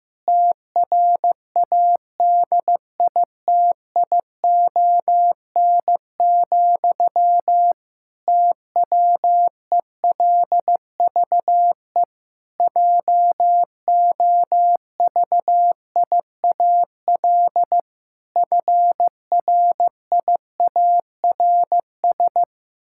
Morse code, heard as a keyed tone, about 15 wpm, 700 Hz. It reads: TRADITION, TWELVE JOVIAL FRIARS
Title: Pangram Challenge v2 - 15wpm